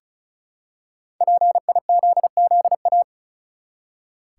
Morse code, heard as a keyed tone, about 34 wpm, 700 Hz. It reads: PIZZA
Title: Five Letter Words - 35wpm